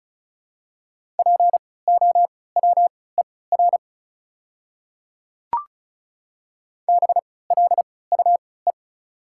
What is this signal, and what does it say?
Morse code, heard as a keyed tone, about 36 wpm, spaced as for 20 wpm, 700 Hz. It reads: POWER BLUE
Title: 3x Character Spacing - Top 500 Words - 35wpm